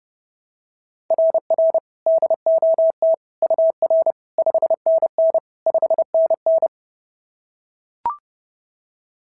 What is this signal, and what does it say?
Morse code, heard as a keyed tone, about 26 wpm, 650 Hz. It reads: RR DOT UR 5NN 5NN
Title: SOTA QSO Practice 30wpm